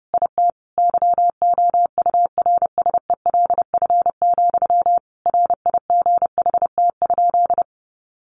Morse code, heard as a keyed tone, about 30 wpm, 700 Hz. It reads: IT YOURSELF, RIGHT?
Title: Sentences from Top 1200 Words 30wpm